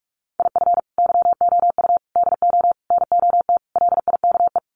Morse code, heard as a keyed tone, about 44 wpm, 700 Hz. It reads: IF YOU DO NOT LIKE
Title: Sets of 5 Words 45wpm